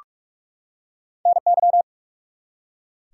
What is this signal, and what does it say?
Morse code, heard as a keyed tone, about 44 wpm, 700 Hz. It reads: NY
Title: US States and Canadian Provinces 45wpm